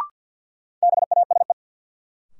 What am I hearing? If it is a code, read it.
Morse code, heard as a keyed tone, about 53 wpm, 700 Hz. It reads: BASE
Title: Sentences from Top 1600 Words - New Word Drill Alt - 50wpm